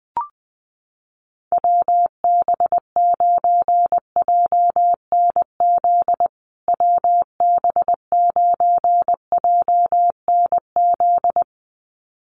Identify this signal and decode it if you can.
Morse code, heard as a keyed tone, about 19 wpm, 700 Hz. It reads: WB9JNZ WB9JNZ
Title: USA Call Signs Repeated 2x 20wpm